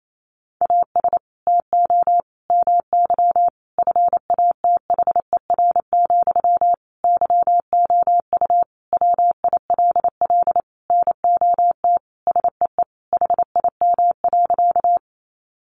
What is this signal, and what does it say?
Morse code, heard as a keyed tone, about 28 wpm, 700 Hz. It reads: AS TO MY FATHER, YOU WILL NOT SEE HIM.
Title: Sentences from Top 600 Words 28wpm